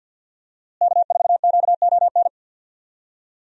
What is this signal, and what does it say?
Morse code, heard as a keyed tone, about 53 wpm, 700 Hz. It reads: K4QON
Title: USA Call Signs - 5 Characters - Repeated 2x - 50wpm